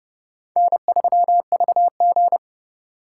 Morse code, heard as a keyed tone, about 36 wpm, 700 Hz. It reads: N3VG
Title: Speed Racing - USA 1x2 Call Signs 30wpm